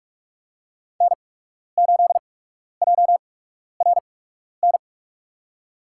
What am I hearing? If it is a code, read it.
Morse code, heard as a keyed tone, about 46 wpm, spaced as for 12 wpm, 700 Hz. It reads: N8JRN
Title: USA Call Signs - 5 Characters - 8x Character Spacing - 45wpm